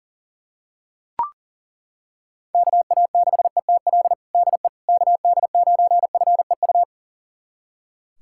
Morse code, heard as a keyed tone, about 40 wpm, 700 Hz. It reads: KA6ETL DE KD9FEU
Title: USA Callsign DE Callsign - 6 Characters - 40wpm